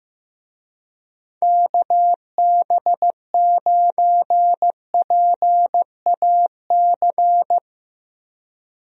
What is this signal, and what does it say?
Morse code, heard as a keyed tone, about 15 wpm, 700 Hz. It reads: KB9PAC
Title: USA Call Signs - 6 Characters - Repeated 2x - 15wpm